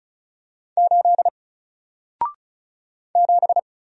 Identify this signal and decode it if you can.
Morse code, heard as a keyed tone, about 36 wpm, 700 Hz. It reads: 8 7
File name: Single-Letter-Number-35wpm